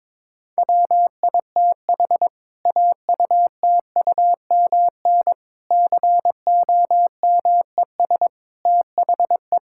Morse code, heard as a keyed tone, about 22 wpm, 700 Hz. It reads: WITH AUTUMN COMES THE
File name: Difficult Sentences with Hints - 22wpm